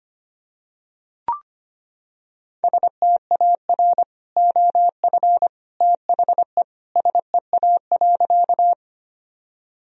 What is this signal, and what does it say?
Morse code, heard as a keyed tone, about 25 wpm, 700 Hz. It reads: STAR OF THE SEA.
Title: Sentences from Top 1000 Words - Review Part 2 - 25wpm